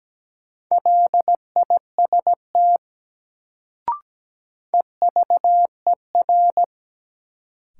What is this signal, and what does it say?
Morse code, heard as a keyed tone, about 17 wpm, 700 Hz. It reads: LIST EVER
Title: Sentences from Top 400 Words - New Word Drill - 17wpm